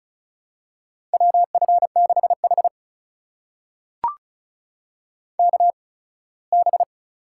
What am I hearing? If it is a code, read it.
Morse code, heard as a keyed tone, about 36 wpm, spaced as for 21 wpm, 700 Hz. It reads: WF6H KB
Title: USA Call Signs - 4 Characters - 8x Character Spacing - 35wpm